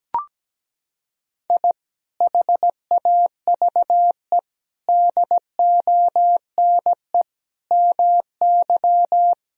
Morse code, heard as a keyed tone, about 17 wpm, 700 Hz. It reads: I HAVE DONE MY
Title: Sentences from Top 1000 Words - Review Part 4 - 17wpm